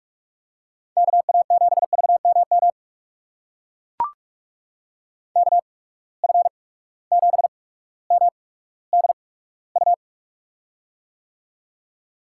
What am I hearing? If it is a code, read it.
Morse code, heard as a keyed tone, about 44 wpm, spaced as for 19 wpm, 700 Hz. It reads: KA7VMM KF7MDU
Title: USA Call Signs - 6 Characters - 8x Character Spacing - 45wpm